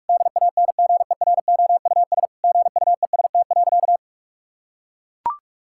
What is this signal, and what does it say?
Morse code, heard as a keyed tone, about 43 wpm, 700 Hz. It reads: DANGEROUS GUEST.
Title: Sentences from Top 1600 Words 45wpm